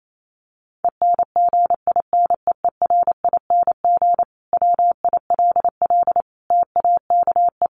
Morse code, heard as a keyed tone, about 28 wpm, 700 Hz. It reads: ENGINEERING WILL TAKE
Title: Sentences from Top 1500 Words 28wpm